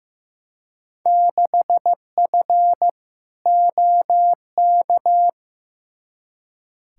Morse code, heard as a keyed tone, about 15 wpm, 700 Hz. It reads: 6F OK
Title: Field Day Exchanges Encore - 15wpm